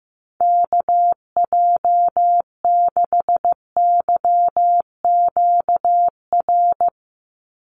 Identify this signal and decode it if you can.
Morse code, heard as a keyed tone, about 15 wpm, 700 Hz. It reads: KJ6YQR
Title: USA Call Signs 15wpm